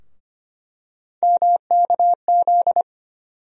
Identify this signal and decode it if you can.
Morse code, heard as a keyed tone, about 25 wpm, 700 Hz. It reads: MKZ
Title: USA Call Sign Postfixes - 3 Letters 25wpm